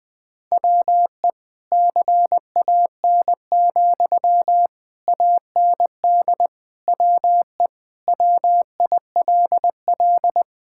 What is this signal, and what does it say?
Morse code, heard as a keyed tone, about 20 wpm, 700 Hz. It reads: WE CAN, AND WE WILL
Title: Sentences from Top 1000 Words - Review Part 6 - 20wpm